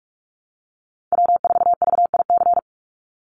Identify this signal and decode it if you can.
Morse code, heard as a keyed tone, about 44 wpm, 700 Hz. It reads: W4VIC
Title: Call Signs Repeated 2x - 45wpm